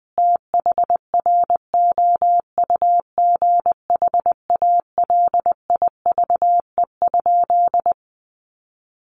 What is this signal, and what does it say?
Morse code, heard as a keyed tone, about 20 wpm, 700 Hz. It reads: THROUGH ALIVE?
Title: A Sound Of Thunder 20wpm